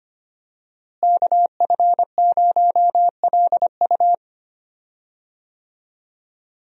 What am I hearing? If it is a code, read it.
Morse code, heard as a keyed tone, about 25 wpm, 700 Hz. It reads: KF0LU
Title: USA Call Signs - 5 Characters - Repeated 3x - 25wpm